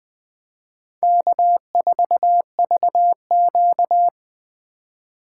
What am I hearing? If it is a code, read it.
Morse code, heard as a keyed tone, about 25 wpm, 700 Hz. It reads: K4VQ
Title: Speed Racing - USA 1x2 Call Signs 20wpm